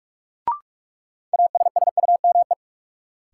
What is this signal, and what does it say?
Morse code, heard as a keyed tone, about 41 wpm, 700 Hz. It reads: ASSUME
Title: Sentences from Top 1500 Words - New Word Drill - 45wpm